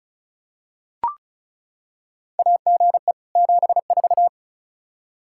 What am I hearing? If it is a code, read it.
Morse code, heard as a keyed tone, about 36 wpm, 700 Hz. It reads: AGE 74
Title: Practice QSO - Exchange 3 - More Info - 35wpm